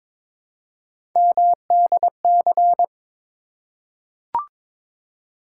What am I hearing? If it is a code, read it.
Morse code, heard as a keyed tone, about 22 wpm, 700 Hz. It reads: MDC
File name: ARRL Sections v2 - 22wpm